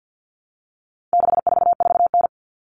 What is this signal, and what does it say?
Morse code, heard as a keyed tone, about 50 wpm, 700 Hz. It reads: 644N
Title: All Cut Numbers - 4-Digit 50wpm